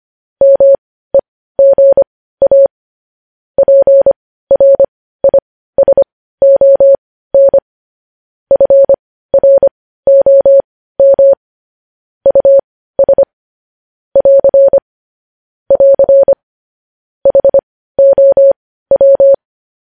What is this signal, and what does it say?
Morse code, heard as a keyed tone, about 25 wpm, spaced as for 15 wpm, 550 Hz. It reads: MEGA PRISON FROM US <AR> <AR> HOW
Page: news-headlines-in-morse-code-at-15-wpm